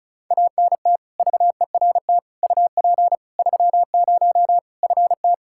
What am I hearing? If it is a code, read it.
Morse code, heard as a keyed tone, about 34 wpm, 700 Hz. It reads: ANT VERT UP 30 FT
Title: K1USN SST QSO Practice 35wpm